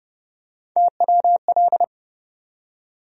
Morse code, heard as a keyed tone, about 30 wpm, 700 Hz. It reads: TWL
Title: USA Call Sign Postfixes - 3 Letters 30wpm